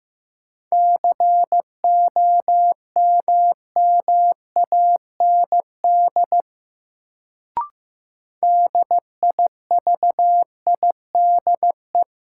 Morse code, heard as a keyed tone, about 15 wpm, 700 Hz. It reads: COMMAND DIVIDE
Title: Sentences from Top 1600 Words - New Word Drill Alt - 15wpm